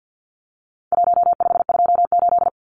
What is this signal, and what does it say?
Morse code, heard as a keyed tone, about 50 wpm, 700 Hz. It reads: 1528
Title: Numbers - Contest Style - 4-Digit 50wpm